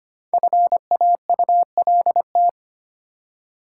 Morse code, heard as a keyed tone, about 25 wpm, 700 Hz. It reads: FAULT
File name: 5 Letter Words - First 3 Letters Given - 25wpm